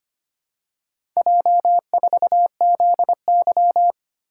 Morse code, heard as a keyed tone, about 25 wpm, 700 Hz. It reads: J4ZY
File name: practice / 25wpm / Four Letter-Number Combinations - 25wpm